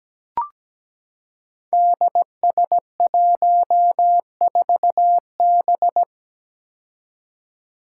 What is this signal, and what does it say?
Morse code, heard as a keyed tone, about 17 wpm, 700 Hz. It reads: DS14B
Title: Five Challenging Letter-Number Combinations 17wpm